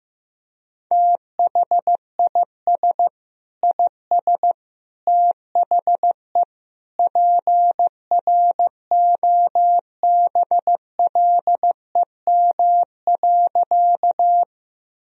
Morse code, heard as a keyed tone, about 15 wpm, 700 Hz. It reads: THIS IS THE PROBLEM.
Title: Sentences from Top 1000 Words - Review Part 2 - 15wpm